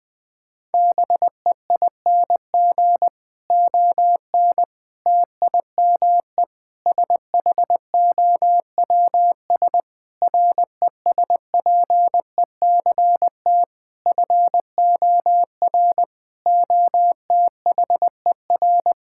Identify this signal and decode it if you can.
Morse code, heard as a keyed tone, about 20 wpm, 700 Hz. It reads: BEING ON TIME SHOWS RESPECT FOR OTHER
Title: Difficult Sentences with Hints - 20wpm